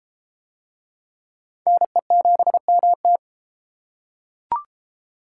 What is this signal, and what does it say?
Morse code, heard as a keyed tone, about 32 wpm, 700 Hz. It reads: NE7MT